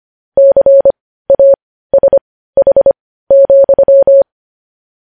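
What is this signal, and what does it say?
Morse code, heard as a keyed tone, about 25 wpm, spaced as for 15 wpm, 550 Hz. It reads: CASH,
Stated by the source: Morse news headlines sent at 15 WPM, beginning with vvv.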